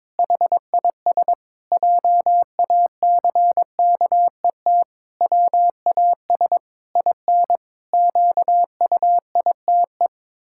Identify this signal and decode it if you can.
Morse code, heard as a keyed tone, about 22 wpm, 700 Hz. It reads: HIS JACKET WAS IN QUITE
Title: Pangram Challenge v2 - 22wpm